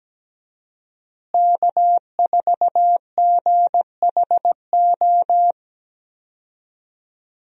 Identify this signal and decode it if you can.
Morse code, heard as a keyed tone, about 17 wpm, 700 Hz. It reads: K4GHO
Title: USA Call Signs - 5 Characters - Repeated 2x - 17wpm